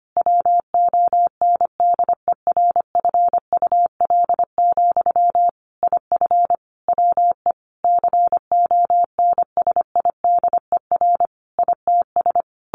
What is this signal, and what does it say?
Morse code, heard as a keyed tone, about 25 wpm, 700 Hz. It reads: WONDERFUL, IF WE CONSIDER ITS
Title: Sentences from Top 1100 Words 25wpm